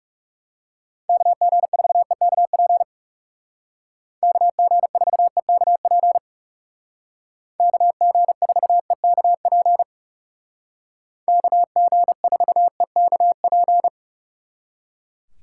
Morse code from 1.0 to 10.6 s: KG4EKP KG4EKP KG4EKP

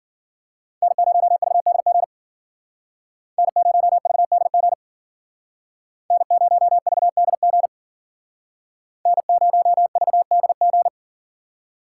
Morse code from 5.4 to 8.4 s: N0VDG

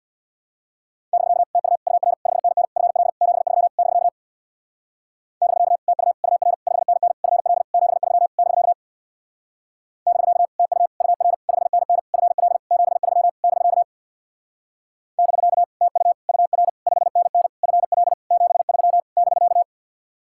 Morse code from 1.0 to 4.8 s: <BK> TU UR 5NN FL 73 <BK>

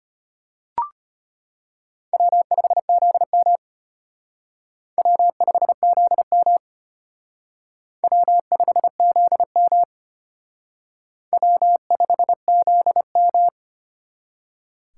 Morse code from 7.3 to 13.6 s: W5ZM W5ZM

